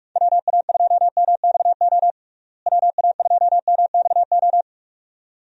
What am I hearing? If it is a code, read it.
Morse code, heard as a keyed tone, about 41 wpm, 700 Hz. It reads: WA2MXO WA2MXO
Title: USA Call Signs - 6 Characters - Repeated 2x - 45wpm